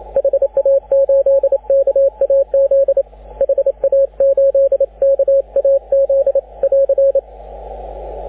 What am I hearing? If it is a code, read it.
Morse code, heard as a keyed tone, about 28 wpm, 550 Hz, with radio noise. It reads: HA8KAZ HA8KAZ <AR>
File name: AUDIO2019 / HG6N CW SSB MIX